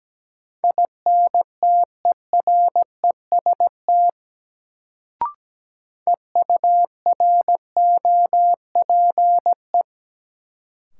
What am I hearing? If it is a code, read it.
Morse code, heard as a keyed tone, about 17 wpm, 700 Hz. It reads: INTEREST EUROPE
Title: Sets of 5 Words Encore - New Word Drill - 17wpm